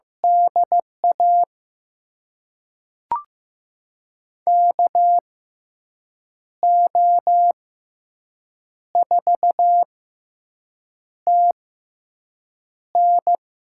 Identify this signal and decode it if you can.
Morse code, heard as a keyed tone, about 15 wpm, spaced as for 6 wpm, 700 Hz. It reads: DA KO4TN
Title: USA Call Signs - 6x Character Spacing - 15wpm